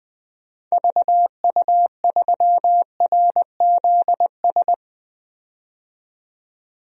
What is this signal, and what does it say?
Morse code, heard as a keyed tone, about 20 wpm, 700 Hz. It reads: VU3RZS
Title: DXCC-100 Call Signs - 6 Characters - 20wpm